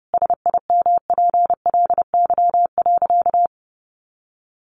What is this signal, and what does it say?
Morse code, heard as a keyed tone, about 30 wpm, 700 Hz. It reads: SIMPLY.
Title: Sentences from Top 600 Words 30wpm